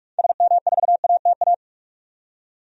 Morse code, heard as a keyed tone, about 46 wpm, 700 Hz. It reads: SM4ATA
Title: DXCC-100 Call Signs - 6 Characters - 45wpm